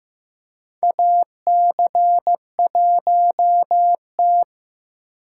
Morse code, heard as a keyed tone, about 15 wpm, 700 Hz. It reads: AC1T
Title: USA Call Signs - 6x Character Spacing - 15wpm